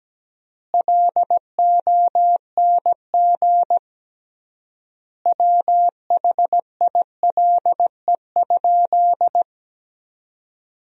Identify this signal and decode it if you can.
Morse code, heard as a keyed tone, about 17 wpm, 700 Hz. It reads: LONG WHILE?
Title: 3x Word-Spacing - Sentences from Top 300 Words - 17wpm